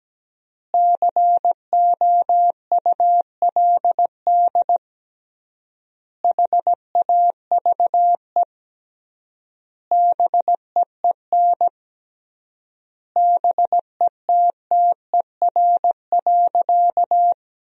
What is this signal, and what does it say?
Morse code, heard as a keyed tone, about 17 wpm, 700 Hz. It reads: COULD HAVE BEEN BETTER.
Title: 3x Word-Spacing - Sentences from Top 500 Words - 17wpm